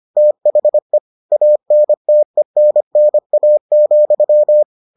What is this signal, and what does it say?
Morse code, heard as a keyed tone, about 25 wpm, 600 Hz. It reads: THE ANTENNA,